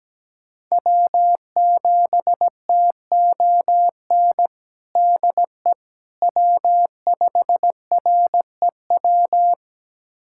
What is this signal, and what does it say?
Morse code, heard as a keyed tone, about 17 wpm, 700 Hz. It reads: W7TON DE W5REW
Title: USA Callsign DE Callsign - 5 Characters - 17wpm